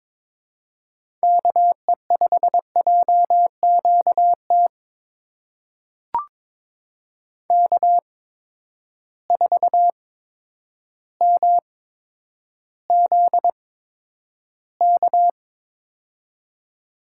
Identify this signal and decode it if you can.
Morse code, heard as a keyed tone, about 22 wpm, spaced as for 10 wpm, 700 Hz. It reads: KE5JQT K4MZK
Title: USA Call Signs - 8x Character Spacing - 22wpm